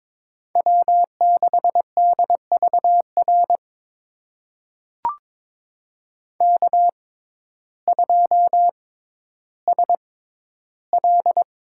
Morse code, heard as a keyed tone, about 22 wpm, spaced as for 12 wpm, 700 Hz. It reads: W6DVR K2SL
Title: USA Call Signs - 5 Characters - 6x Character Spacing - 22wpm